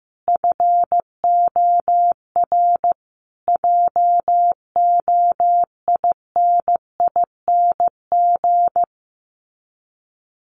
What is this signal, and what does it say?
Morse code, heard as a keyed tone, about 15 wpm, 700 Hz. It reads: FOR JOINING
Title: Sets of 5 Words Encore2 15wpm